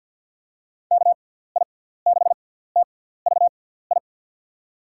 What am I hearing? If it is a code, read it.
Morse code, heard as a keyed tone, about 47 wpm, spaced as for 17 wpm, 700 Hz. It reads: KI6TVI
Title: USA Call Signs - 6 Characters - 6x Character Spacing - 50wpm